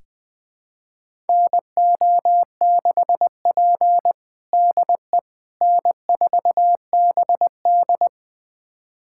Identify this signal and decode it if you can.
Morse code, heard as a keyed tone, about 20 wpm, 700 Hz. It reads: NO6P DE N4BD
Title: USA Callsign DE Callsign - 4 Characters - 20wpm